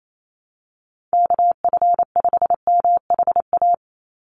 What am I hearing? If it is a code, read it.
Morse code, heard as a keyed tone, about 28 wpm, 700 Hz. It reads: KF5MHA
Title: USA Call Signs 28wpm